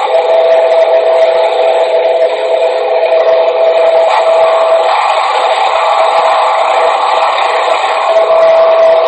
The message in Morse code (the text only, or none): none